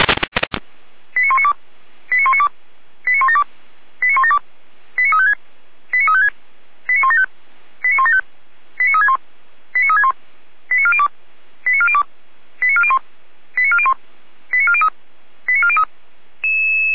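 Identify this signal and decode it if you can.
Five-tone selective call (ZVEI), heard as quick runs of five tones, tones of about 70 ms. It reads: 89192 89192 89172 89172 89356 89356 89175 89175 89361 89361 89402 89402 89401 89401 89403 89403
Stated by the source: ZVEI five-tone call received by SDR.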